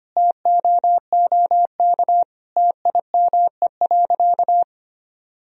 Morse code, heard as a keyed tone, about 25 wpm, 700 Hz. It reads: TOOK TIME.
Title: Sentences from Top 1000 Words - Review Part 4 - 25wpm